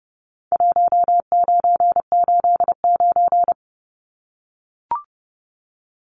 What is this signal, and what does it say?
Morse code, heard as a keyed tone, about 29 wpm, 700 Hz. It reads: 1989
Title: Numbers - 4-Digit 30wpm